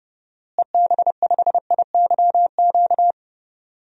Morse code, heard as a keyed tone, about 30 wpm, 700 Hz. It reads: EB5IYQ
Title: DXCC-100 Call Signs - 6 Characters - Repeated 2x - 30wpm